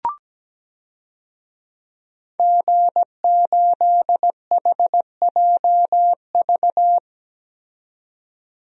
Morse code, heard as a keyed tone, about 17 wpm, 700 Hz. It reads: G8HJV
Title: DXCC-100 Call Signs - 5 Characters - 17wpm